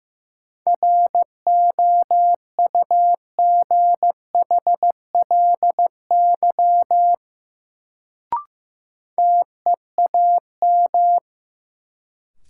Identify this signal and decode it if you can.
Morse code, heard as a keyed tone, about 15 wpm, 700 Hz. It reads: ROUGHLY TEAM
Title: Sentences from Top 1500 Words - New Word Drill - 15wpm